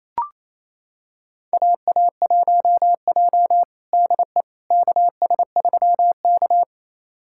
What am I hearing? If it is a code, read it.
Morse code, heard as a keyed tone, about 28 wpm, 700 Hz. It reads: AA1J DE KS3K
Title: USA Callsign DE Callsign - 4 Characters - 28wpm